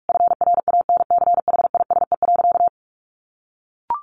Morse code, heard as a keyed tone, about 44 wpm, 700 Hz. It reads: FRANCHISE.
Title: A Sound Of Thunder 45wpm